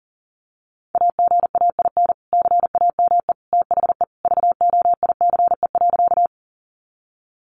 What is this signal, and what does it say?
Morse code, heard as a keyed tone, about 40 wpm, 700 Hz. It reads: AGAIN CAME THE VOICE.
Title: Sentences from Top 500 Words 40wpm